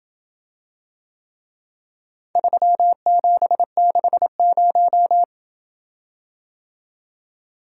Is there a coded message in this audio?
Yes: Morse code from 2.4 to 5.2 s, heard as a keyed tone, about 27 wpm, 700 Hz. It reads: 3760